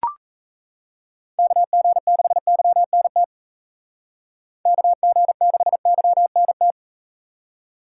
Morse code, heard as a keyed tone, about 41 wpm, 700 Hz. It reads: KG6YNT KG6YNT